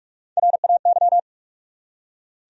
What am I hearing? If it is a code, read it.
Morse code, heard as a keyed tone, about 46 wpm, 700 Hz. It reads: RAY
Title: USA Common Names - Max 3 Letters - 45wpm